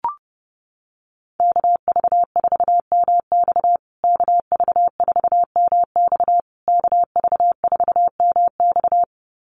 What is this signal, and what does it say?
Morse code, heard as a keyed tone, about 30 wpm, 700 Hz. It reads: KV4MX KV4MX KV4MX
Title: USA Call Signs Repeated 3x 30wpm